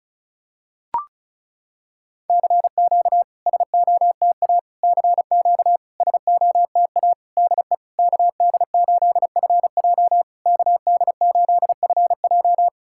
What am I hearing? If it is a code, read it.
Morse code, heard as a keyed tone, about 36 wpm, 690 Hz. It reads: CQ SOTA CQ SOTA DE KD8FJ KD8FJ
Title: SOTA QSO Practice 35wpm